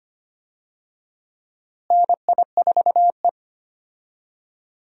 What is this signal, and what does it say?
Morse code, heard as a keyed tone, about 25 wpm, 700 Hz. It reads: NI4E
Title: USA Call Signs - 4 Characters - Repeated 2x - 25wpm